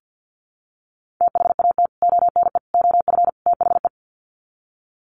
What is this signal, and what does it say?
Morse code, heard as a keyed tone, about 50 wpm, 700 Hz. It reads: THAT ONE OF THE
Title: Sets of 4 Words 50wpm